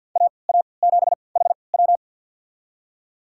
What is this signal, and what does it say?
Morse code, heard as a keyed tone, about 51 wpm, spaced as for 28 wpm, 700 Hz. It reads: AA7HW
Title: USA Call Signs - 5 Characters - 3x Character Spacing - 50wpm